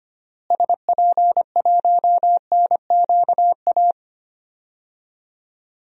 Morse code, heard as a keyed tone, about 25 wpm, 700 Hz. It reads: SP1NQA
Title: DXCC-100 Call Signs - 6 Characters - Repeated 3x - 25wpm